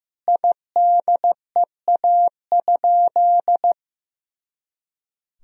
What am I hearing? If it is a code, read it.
Morse code, heard as a keyed tone, about 16 wpm, 700 Hz. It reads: IDEA?
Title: Sentences from Top 400 Words and Abbreviations 15wpm